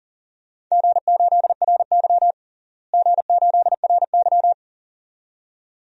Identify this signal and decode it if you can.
Morse code, heard as a keyed tone, about 37 wpm, 700 Hz. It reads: G8RY G8RY
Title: DXCC-100 Call Signs - 4 Characters - Repeated 2x - 40wpm